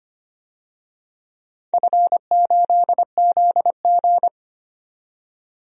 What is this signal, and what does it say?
Morse code, heard as a keyed tone, about 25 wpm, 700 Hz. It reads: F8ZG
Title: DXCC-100 Call Signs - 4 Characters - 25wpm